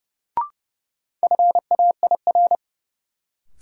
Morse code, heard as a keyed tone, about 31 wpm, 700 Hz. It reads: FAIR
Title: Sentences from Top 1300 Words - New Word Drill Alt - 30wpm